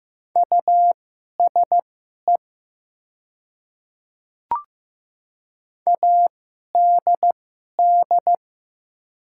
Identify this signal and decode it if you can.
Morse code, heard as a keyed tone, about 15 wpm, spaced as for 11 wpm, 700 Hz. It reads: USE ADD
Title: 2x Character Spacing - Top 200 Words - 15wpm